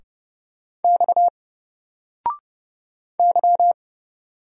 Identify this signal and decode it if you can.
Morse code, heard as a keyed tone, about 30 wpm, 700 Hz. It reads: X Y
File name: Single-Letter-Number-30wpm